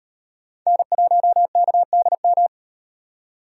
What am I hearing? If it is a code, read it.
Morse code, heard as a keyed tone, about 35 wpm, 700 Hz. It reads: N1KDM